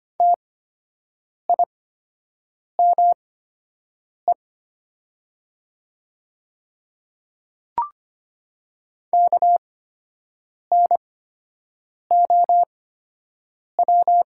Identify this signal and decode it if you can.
Morse code, heard as a keyed tone, about 25 wpm, spaced as for 7 wpm, 700 Hz. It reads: TIME KNOW
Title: 8x Character Spacing - Top 100 Words - 25wpm